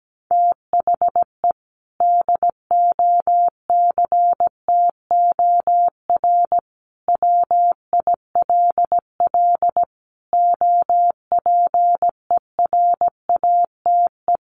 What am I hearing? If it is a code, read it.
Morse code, heard as a keyed tone, about 17 wpm, 700 Hz. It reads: THE DOCTOR WILL OPERATE
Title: Sentences from Top 1500 Words 17wpm